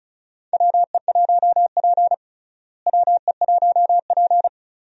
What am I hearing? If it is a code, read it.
Morse code, heard as a keyed tone, about 36 wpm, 700 Hz. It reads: WE1P WE1P
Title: USA Call Signs - 4 Characters - Repeated 3x - 35wpm